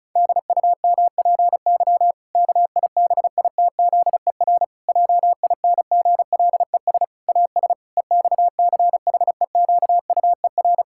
Morse code, heard as a keyed tone, about 36 wpm, 700 Hz. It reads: DUMPY KIBITZER JINGLES AS EXCHEQUER
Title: Pangram Challenge v2 - 35wpm